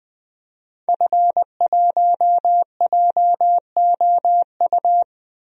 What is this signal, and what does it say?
Morse code, heard as a keyed tone, about 20 wpm, 700 Hz. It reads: F1JOU
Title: DXCC-100 Call Signs - 5 Characters - 20wpm